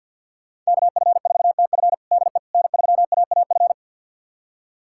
Morse code, heard as a keyed tone, about 51 wpm, 700 Hz. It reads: KF4TF DE N3AAF
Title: USA Callsign DE Callsign - 5 Characters - 50wpm